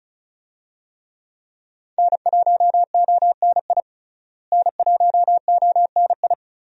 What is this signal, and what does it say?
Morse code, heard as a keyed tone, about 36 wpm, 700 Hz. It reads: N1ONI N1ONI
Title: USA Call Signs - 5 Characters - Repeated 3x - 35wpm